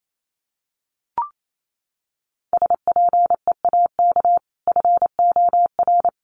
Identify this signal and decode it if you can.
Morse code, heard as a keyed tone, about 28 wpm, 700 Hz. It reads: SPEAK FOR
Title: Sentences from Top 1100 Words 28wpm